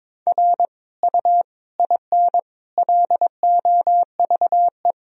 Morse code, heard as a keyed tone, about 22 wpm, 700 Hz. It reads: R U IN LOVE
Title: Sentences from Top 400 Words and Abbreviations 22wpm